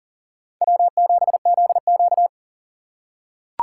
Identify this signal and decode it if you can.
Morse code, heard as a keyed tone, about 40 wpm, 700 Hz. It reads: W7ZQ
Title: USA Call Signs - 4 Characters - 3x Character Spacing - 40wpm